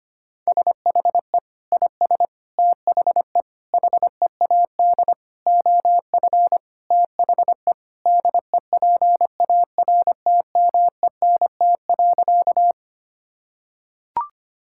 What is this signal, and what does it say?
Morse code, heard as a keyed tone, about 25 wpm, 700 Hz. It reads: SHE IS THE HEAD OF THE DEPARTMENT.
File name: practice / 25wpm / Sentences with 4-Word Phrases - 25wpm